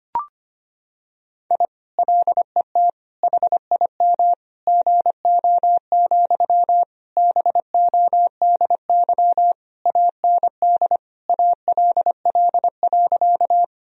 Morse code, heard as a keyed tone, about 25 wpm, 700 Hz. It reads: I LET HIM GO, BODY AND ALL.
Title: Sentences from Top 1000 Words - Review Part 2 - 25wpm